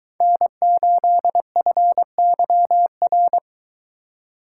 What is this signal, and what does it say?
Morse code, heard as a keyed tone, about 22 wpm, 700 Hz. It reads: N8FYR